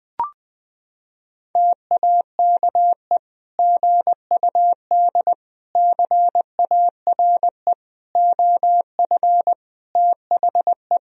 Morse code, heard as a keyed tone, about 20 wpm, 700 Hz. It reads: TAKE GUD CARE OF THE
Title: Sentences from Top 400 Words and Abbreviations 20wpm